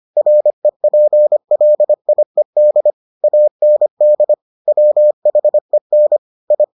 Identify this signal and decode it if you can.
Morse code, heard as a keyed tone, about 25 wpm, 600 Hz. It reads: REPLIED AND WHEN I